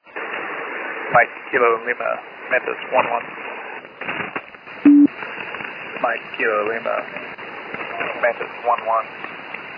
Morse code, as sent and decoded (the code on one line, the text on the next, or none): none